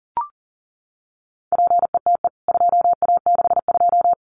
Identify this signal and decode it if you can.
Morse code, heard as a keyed tone, about 40 wpm, 700 Hz. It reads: PETE 2A62
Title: CWT Exchange 40wpm